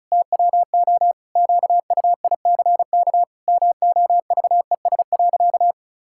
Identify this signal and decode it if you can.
Morse code, heard as a keyed tone, about 34 wpm, 700 Hz. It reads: TWO QUICK MOVES.
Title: Pangram Challenge v2 - 35wpm